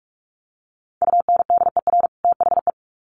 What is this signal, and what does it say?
Morse code, heard as a keyed tone, about 44 wpm, 700 Hz. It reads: UNDER THE
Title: Sets of 2 Words 45wpm